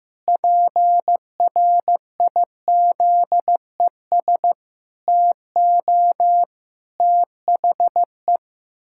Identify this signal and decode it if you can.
Morse code, heard as a keyed tone, about 16 wpm, 700 Hz. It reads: PRIZES TO THE
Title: Pangram Challenge v2 - 15wpm